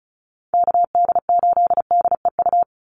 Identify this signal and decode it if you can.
Morse code, heard as a keyed tone, about 34 wpm, 700 Hz. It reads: KD8DEU
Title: NA SOTA Chaser Callsigns 35wpm